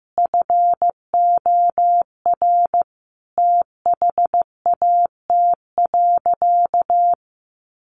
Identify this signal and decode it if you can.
Morse code, heard as a keyed tone, about 14 wpm, 700 Hz. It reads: FOR THAT.
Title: Sentences from Top 1100 Words 15wpm